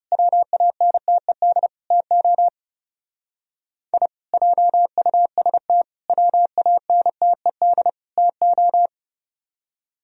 Morse code, heard as a keyed tone, about 36 wpm, 700 Hz. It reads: WANTED TO I JUST WANTED TO
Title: Speed-Racing - Sets of 4 Words 30wpm